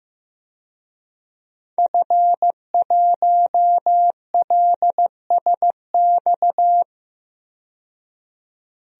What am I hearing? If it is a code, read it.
Morse code, heard as a keyed tone, about 15 wpm, 700 Hz. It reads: F1LSX
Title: DXCC-100 Call Signs - 5 Characters - 15wpm